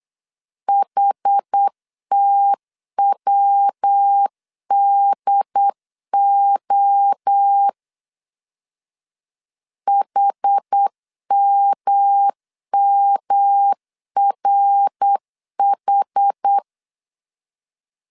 Morse code, heard as a keyed tone, about 8 wpm, 800 Hz. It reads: HTWDO HMMRH
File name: CW / DET / 1 3 1500C S B 8wpm 4141